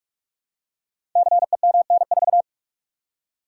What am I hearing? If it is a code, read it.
Morse code, heard as a keyed tone, about 44 wpm, 700 Hz. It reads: CEMN4
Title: Code Groups - 5 Letter:Numbers - 45wpm